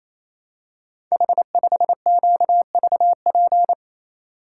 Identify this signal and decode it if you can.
Morse code, heard as a keyed tone, about 28 wpm, 700 Hz. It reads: H5QVP
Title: Code Groups - 5 Letter:Numbers - 28wpm